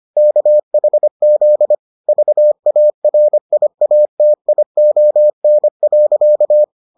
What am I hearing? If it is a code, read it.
Morse code, heard as a keyed tone, about 25 wpm, 600 Hz. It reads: KHZ VARIATION.